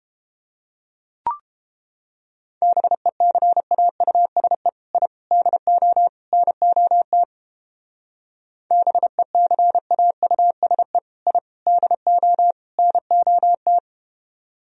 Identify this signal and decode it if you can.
Morse code, heard as a keyed tone, about 30 wpm, 700 Hz. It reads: BECAUSE I DO NOT BECAUSE I DO NOT